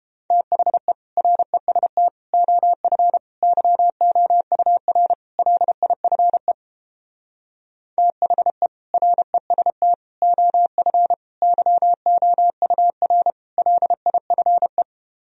Morse code, heard as a keyed tone, about 31 wpm, 700 Hz. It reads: THE REST OF YOUR LIFE THE REST OF YOUR LIFE